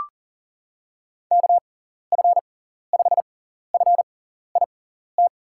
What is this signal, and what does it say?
Morse code, heard as a keyed tone, about 37 wpm, spaced as for 14 wpm, 700 Hz. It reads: KF5FIT
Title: USA Call Signs - 6 Characters - 6x Character Spacing - 40wpm